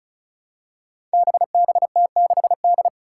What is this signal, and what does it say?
Morse code, heard as a keyed tone, about 36 wpm, 700 Hz. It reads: BBT6D
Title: Five Challenging Letter-Number Combinations 35wpm